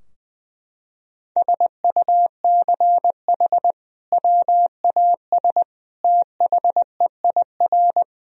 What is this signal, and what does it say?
Morse code, heard as a keyed tone, about 20 wpm, 700 Hz. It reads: SUCH WAS THEIR
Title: Sentences from Top 1000 Words - Review Part 2 - 20wpm